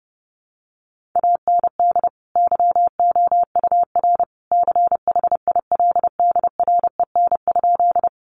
Morse code, heard as a keyed tone, about 30 wpm, 700 Hz. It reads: AND YOUR CHILDREN?
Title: Sentences from Top 300 Words 30wpm